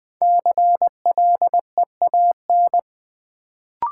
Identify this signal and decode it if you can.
Morse code, heard as a keyed tone, about 20 wpm, 700 Hz. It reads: CLEAN
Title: Sentences from Top 1100 Words - New Word Drill - 20wpm